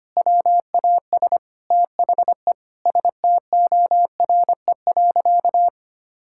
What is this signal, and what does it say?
Morse code, heard as a keyed tone, about 25 wpm, 700 Hz. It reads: WAS THE STORE.
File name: practice / 25wpm / Sentences with 4-Word Phrases - 25wpm